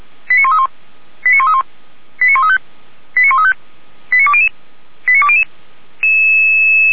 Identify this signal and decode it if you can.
Five-tone selective call (ZVEI), heard as quick runs of five tones, tones of about 70 ms. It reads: 89131 89131 89135 89135 89200 89200